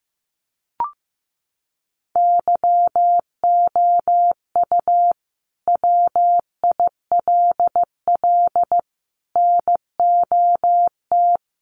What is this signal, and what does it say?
Morse code, heard as a keyed tone, about 15 wpm, 700 Hz. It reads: YOU WILL NOT
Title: Sentences from Top 1300 Words 15wpm